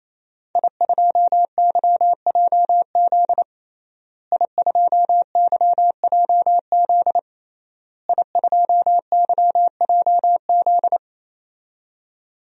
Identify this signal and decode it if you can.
Morse code, heard as a keyed tone, about 28 wpm, 700 Hz. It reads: I2YJZ I2YJZ I2YJZ
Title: DXCC-100 Call Signs - 5 Characters - Repeated 3x - 28wpm